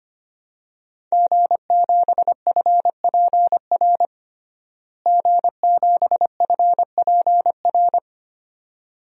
Morse code, heard as a keyed tone, about 25 wpm, 700 Hz. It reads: G7FPR G7FPR
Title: DXCC-100 Call Signs - 5 Characters - Repeated 2x - 25wpm